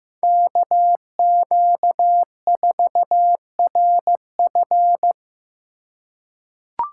Morse code, heard as a keyed tone, about 15 wpm, 700 Hz. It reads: KQ4RF
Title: USA Call Signs - 4x Character Spacing - 15wpm